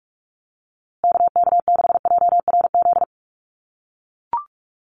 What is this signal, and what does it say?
Morse code, heard as a keyed tone, about 44 wpm, 700 Hz. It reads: KK6JRZ
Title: USA Call Signs Repeated 2x 45wpm